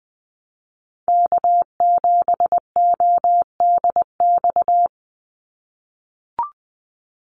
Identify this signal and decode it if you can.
Morse code, heard as a keyed tone, about 20 wpm, 700 Hz. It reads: K7ODX
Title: NA SOTA Activator Callsigns 20wpm